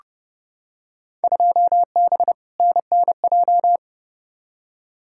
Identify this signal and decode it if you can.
Morse code, heard as a keyed tone, about 30 wpm, 700 Hz. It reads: 2B NNJ
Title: Field Day Exchanges Encore - 30wpm